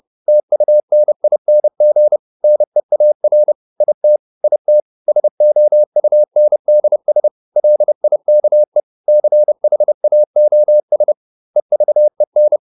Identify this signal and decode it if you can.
Morse code, heard as a keyed tone, about 30 wpm, 600 Hz. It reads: TUNING NEAR IT IT SOUNDS LIKE CHAOS EVEN